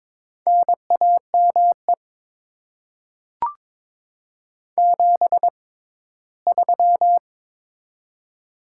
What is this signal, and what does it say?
Morse code, heard as a keyed tone, about 22 wpm, spaced as for 15 wpm, 700 Hz. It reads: NAME 73
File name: Top 20 QSO Elements - 6x Character Spacing - 22wpm